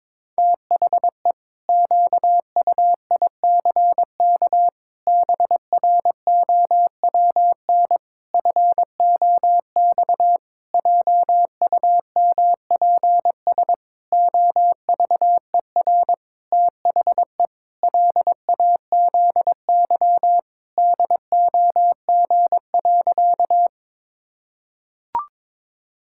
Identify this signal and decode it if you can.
Morse code, heard as a keyed tone, about 22 wpm, 700 Hz. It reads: THE QUICK BROWN FOX JUMPS OVER THE LAZY DOG.
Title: Pangram Challenge v2 - 22wpm